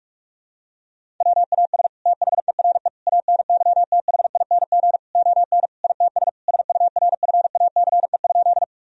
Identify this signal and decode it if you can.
Morse code, heard as a keyed tone, about 46 wpm, 700 Hz. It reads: WAS THERE ANYTHING ON ITS SURFACE?
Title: Sentences from Top 1000 Words - Review Part 3 - 45wpm